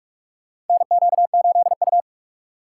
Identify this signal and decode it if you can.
Morse code, heard as a keyed tone, about 46 wpm, 700 Hz. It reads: NQ8U
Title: USA Call Signs - 4x Character Spacing - 45wpm